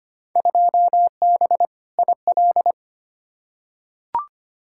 Morse code, heard as a keyed tone, about 25 wpm, 700 Hz. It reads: 2B IL
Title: Field Day Exchanges Encore - 25wpm